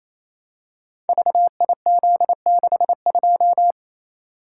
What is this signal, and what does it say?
Morse code, heard as a keyed tone, about 28 wpm, 700 Hz. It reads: VIZ62
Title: Code Groups - 5 Letter:Numbers - 28wpm